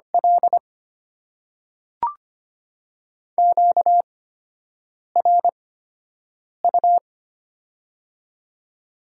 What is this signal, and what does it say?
Morse code, heard as a keyed tone, about 25 wpm, spaced as for 7 wpm, 700 Hz. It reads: L QRU
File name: Top 90 QSO Elements - 8x Character Spacing - 25wpm